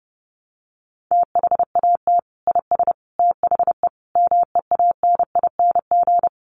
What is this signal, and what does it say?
Morse code, heard as a keyed tone, about 30 wpm, 700 Hz. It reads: THAT IS THE MEANING
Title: Sentences from Top 900 Words 30wpm